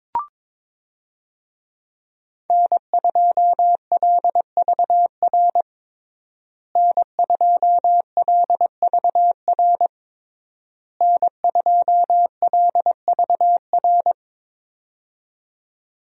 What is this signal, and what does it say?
Morse code, heard as a keyed tone, about 22 wpm, 700 Hz. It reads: N2LVR N2LVR N2LVR
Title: USA Call Signs - 5 Characters - Repeated 3x - 22wpm